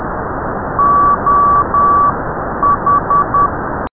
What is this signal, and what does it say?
Morse code, heard as a keyed tone, about 10 wpm, spaced as for 9 wpm, 1140 Hz, with noise. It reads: OH
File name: beacon / oh6shf / 1536z oh6shf 13cm by oh6nvq